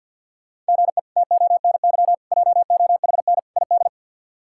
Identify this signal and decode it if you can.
Morse code, heard as a keyed tone, about 51 wpm, 700 Hz. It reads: GE TONY JOHN ID
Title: K1USN SST QSO Practice - 50wpm